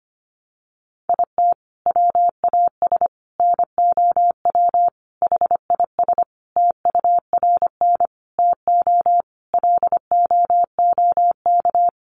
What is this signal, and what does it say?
Morse code, heard as a keyed tone, about 25 wpm, 700 Hz. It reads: IT WAS NOW HIS TURN TO LOOK
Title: Sentences from Top 200 Words 25wpm